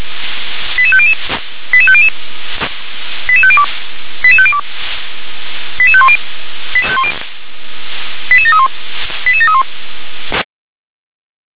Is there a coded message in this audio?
Yes: five-tone selective call (ZVEI) from 0.8 to 9.6 s, heard as quick runs of five tones, tones of about 70 ms. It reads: 88500 88500 88502 88502 88510 88731 88731